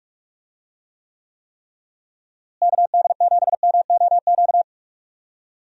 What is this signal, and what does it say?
Morse code, heard as a keyed tone, about 46 wpm, 700 Hz. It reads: KD7MOQ
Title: USA Call Signs - 6 Characters - Repeated 3x - 45wpm